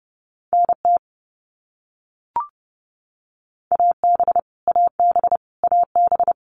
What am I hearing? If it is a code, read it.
Morse code, heard as a keyed tone, about 30 wpm, 700 Hz. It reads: NT AB AB AB
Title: Canadian Provinces and Territories - Repeated 3x - 30wpm